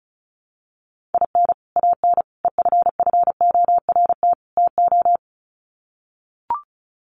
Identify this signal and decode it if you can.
Morse code, heard as a keyed tone, about 34 wpm, 700 Hz. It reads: IN AN EFFORT TO
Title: Sets of 4 Words 35wpm